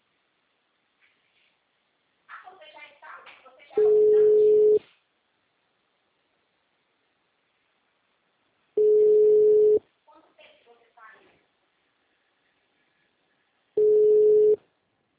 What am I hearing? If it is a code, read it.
Ringback tone (Europe, 425 Hz), with speech.